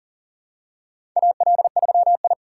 Morse code, heard as a keyed tone, about 38 wpm, 700 Hz. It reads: AL3I
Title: USA Call Signs - 4 Characters - 6x Character Spacing - 40wpm